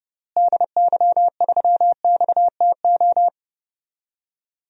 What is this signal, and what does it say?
Morse code, heard as a keyed tone, about 30 wpm, 700 Hz. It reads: DY3XTO
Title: DXCC-100 Call Signs - 6 Characters - Repeated 3x - 30wpm